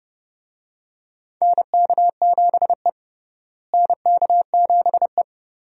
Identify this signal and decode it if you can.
Morse code, heard as a keyed tone, about 30 wpm, 700 Hz. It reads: NK7E NK7E
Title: USA Call Signs - 4 Characters - Repeated 2x - 30wpm